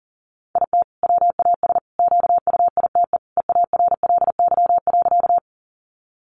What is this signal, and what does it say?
Morse code, heard as a keyed tone, about 40 wpm, 700 Hz. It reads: IT WAS QUITE EARLY.
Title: Sentences from Top 700 Words 40wpm